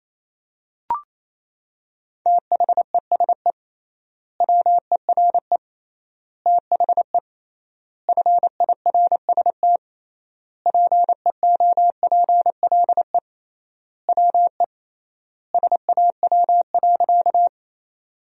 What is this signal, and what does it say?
Morse code, heard as a keyed tone, about 27 wpm, 700 Hz. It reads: THESE WERE THE FIRST PEOPLE WE SAW.
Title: 3x Word-Spacing - Sentences from Top 300 Words - 28wpm